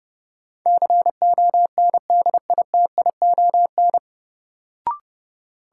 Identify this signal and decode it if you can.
Morse code, heard as a keyed tone, about 30 wpm, 700 Hz. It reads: CONDITION
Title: Sentences from Top 1000 Words - New Word Drill - 30wpm